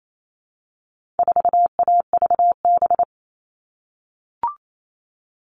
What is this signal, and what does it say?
Morse code, heard as a keyed tone, about 28 wpm, 700 Hz. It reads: 4AVB
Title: All Cut Numbers - 4-Digit 28wpm